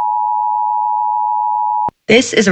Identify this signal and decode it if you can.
Emergency Alert System (SAME) tones (the 853 and 960 Hz attention tone), with speech. It is heard from 0.0 to 1.9 s.